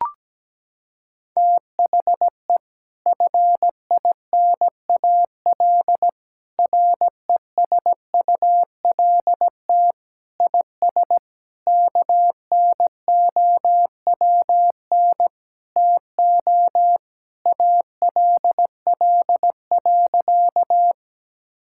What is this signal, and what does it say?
Morse code, heard as a keyed tone, about 17 wpm, 700 Hz. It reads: THE FINAL RESULT IS KNOWN TO ALL.
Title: Sentences from Top 1000 Words - Review Part 5 - 17wpm